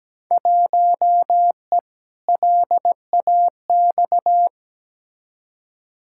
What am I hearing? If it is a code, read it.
Morse code, heard as a keyed tone, about 17 wpm, 700 Hz. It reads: 1E LAX
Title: Field Day Exchanges - 17wpm